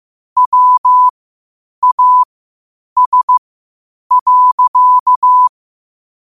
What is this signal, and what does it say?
Morse code, heard as a keyed tone, about 15 wpm, spaced as for 8 wpm, 1000 Hz. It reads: WAS.
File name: qotdfiles / 8wpm